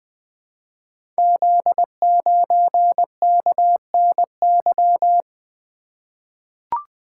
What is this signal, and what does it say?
Morse code, heard as a keyed tone, about 21 wpm, 700 Hz. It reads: Z9KNY
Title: Code Groups - 5 Letter:Numbers - 20wpm